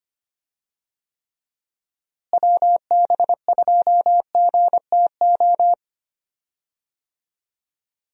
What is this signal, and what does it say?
Morse code, heard as a keyed tone, about 25 wpm, 700 Hz. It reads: WB2GTO
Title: USA Call Signs - 6 Characters - Repeated 3x - 25wpm